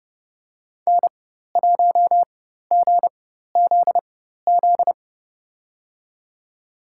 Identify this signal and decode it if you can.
Morse code, heard as a keyed tone, about 30 wpm, spaced as for 14 wpm, 690 Hz. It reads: N1GZZ
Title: USA Call Signs - 4x Character Spacing - 30wpm